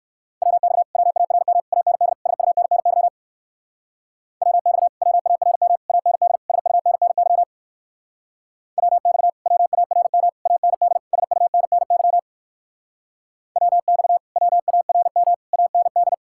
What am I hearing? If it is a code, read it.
Morse code, heard as a keyed tone, about 59 wpm, 700 Hz. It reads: WX WARM AND SUNNY WX WARM AND SUNNY WX WARM AND SUNNY WX WARM AND